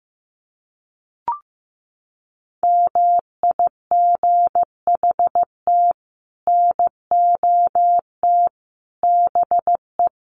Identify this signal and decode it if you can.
Morse code, heard as a keyed tone, about 15 wpm, 700 Hz. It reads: MIGHT NOT BE
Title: Sets of 5 Words Encore2 15wpm